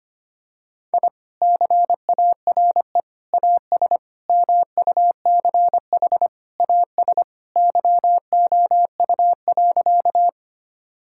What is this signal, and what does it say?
Morse code, heard as a keyed tone, about 25 wpm, 700 Hz. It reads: I CARE AS MUCH AS YOU.
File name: practice / 25wpm / Sentences with 3-Word Phrases - 25wpm